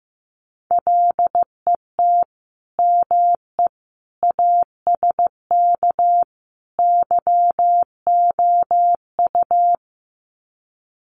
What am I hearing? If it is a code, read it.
Morse code, heard as a keyed tone, about 15 wpm, 700 Hz. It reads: LET ME ASK YOU
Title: Sets of 4 Words 15wpm